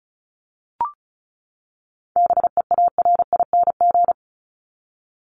Morse code, heard as a keyed tone, about 34 wpm, 700 Hz. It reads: BEARING
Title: ING Suffix - Top 100 Words with Base Words 35wpm